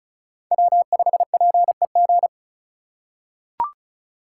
Morse code, heard as a keyed tone, about 36 wpm, 700 Hz. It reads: W5PEG
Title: USA Call Signs - 3x Character Spacing - 35wpm